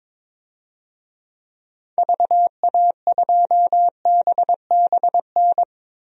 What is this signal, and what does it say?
Morse code, heard as a keyed tone, about 22 wpm, 700 Hz. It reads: VA2BBN
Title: DXCC-100 Call Signs - 6 Characters - 22wpm